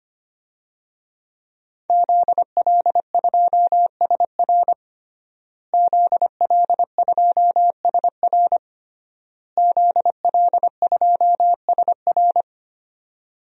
Morse code, heard as a keyed tone, about 25 wpm, 700 Hz. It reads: ZL2SR ZL2SR ZL2SR
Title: DXCC-100 Call Signs - 5 Characters - Repeated 3x - 25wpm